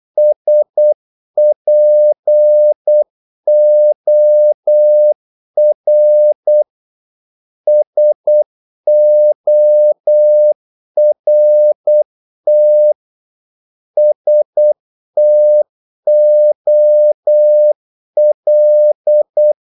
Morse code, heard as a keyed tone, about 8 wpm, 600 Hz. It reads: SPOR SORT STOL
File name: Korte ord DK 8wpm